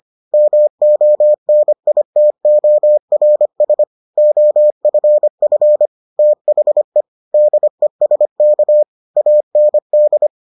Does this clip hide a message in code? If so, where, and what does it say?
Yes: Morse code from 0.3 to 10.3 s, heard as a keyed tone, about 25 wpm, 600 Hz. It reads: MONITORS OFF THE DESK AND